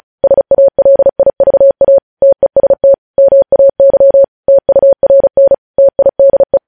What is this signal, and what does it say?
Morse code, heard as a keyed tone, about 36 wpm, 550 Hz. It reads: SALIVA TEST MAY TURN TIDE
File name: podcast / News headlines-1x-35x35